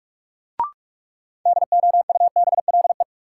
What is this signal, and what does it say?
Morse code, heard as a keyed tone, about 45 wpm, 700 Hz. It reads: DOUBLE
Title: Sentences from Top 1400 Words - New Word Drill Alt - 45wpm